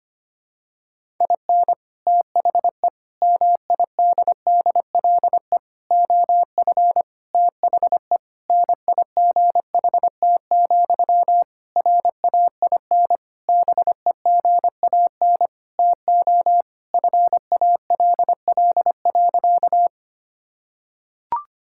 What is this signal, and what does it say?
Morse code, heard as a keyed tone, about 25 wpm, 700 Hz. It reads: IN THE MIDDLE OF THE NIGHT, RAIN BEGAN TO FALL.
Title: Sentences from Top 1000 Words - Review Part 6 - 25wpm